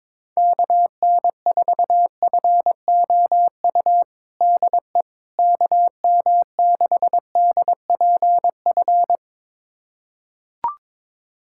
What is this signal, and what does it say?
Morse code, heard as a keyed tone, about 22 wpm, 700 Hz. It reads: KN4FOU DE KM6DPF
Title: USA Callsign DE Callsign - 6 Characters - 22wpm